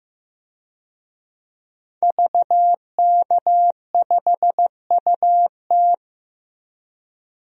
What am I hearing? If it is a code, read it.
Morse code, heard as a keyed tone, about 15 wpm, 700 Hz. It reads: VK5UT
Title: DXCC-100 Call Signs - 5 Characters - Repeated 2x - 15wpm